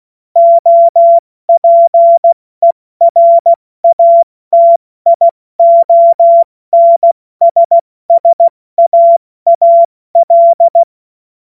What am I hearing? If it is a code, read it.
Morse code, heard as a keyed tone, about 16 wpm, 680 Hz. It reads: OPERATIONSSAAL
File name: Morsekurs / Witze / Witz 20. Juni 2023 70 F80